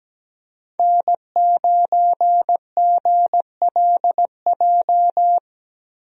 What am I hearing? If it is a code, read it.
Morse code, heard as a keyed tone, about 17 wpm, 700 Hz. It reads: N9GLJ
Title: USA Call Signs - 5 Characters - Repeated 3x - 17wpm